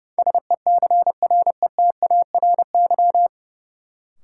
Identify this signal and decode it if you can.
Morse code, heard as a keyed tone, about 30 wpm, 700 Hz. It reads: SECRETARY
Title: Sentences from Top 1300 Words - New Word Drill Alt - 30wpm